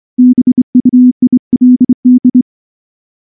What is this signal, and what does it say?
Morse code, heard as a keyed tone, about 24 wpm, 250 Hz. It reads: BUILD
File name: morse / morse-training-25WPM